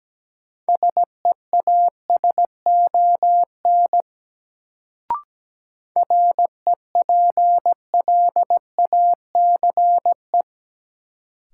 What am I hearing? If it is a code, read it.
Morse code, heard as a keyed tone, about 17 wpm, 700 Hz. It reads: SEASON REPLACE
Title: Sentences from Top 1400 Words - New Word Drill - 17wpm